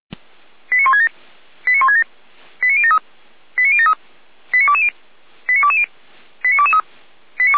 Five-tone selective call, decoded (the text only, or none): 89167 89167 89073 89073 89220 89220 89223